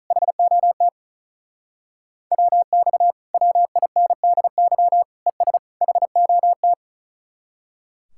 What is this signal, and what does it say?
Morse code, heard as a keyed tone, about 46 wpm, 700 Hz. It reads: HOT WX WINDY ES HOT